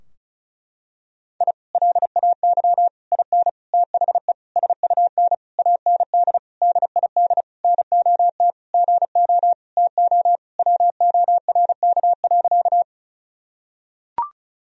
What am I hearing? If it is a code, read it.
Morse code, heard as a keyed tone, about 36 wpm, 700 Hz. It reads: I LAY IN THE SUN AND DID NOT GO TO WORK.
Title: Sentences from Top 1000 Words - Review Part 2 - 35wpm